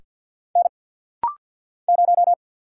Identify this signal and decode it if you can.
Morse code, heard as a keyed tone, about 50 wpm, 700 Hz. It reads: N 0
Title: Single Letter-Number - Rapid-Fire 50wpm